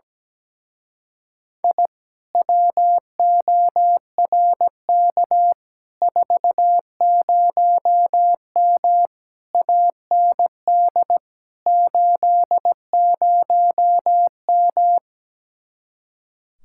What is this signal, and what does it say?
Morse code, heard as a keyed tone, about 17 wpm, 700 Hz. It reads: I WORK 40M AND 80M
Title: Speed-Racing Common QSO Phrases - 17wpm